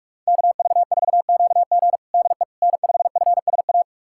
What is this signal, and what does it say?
Morse code, heard as a keyed tone, about 46 wpm, 700 Hz. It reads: KV4QG DE N5FSA
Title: USA Callsign DE Callsign - 5 Characters - 45wpm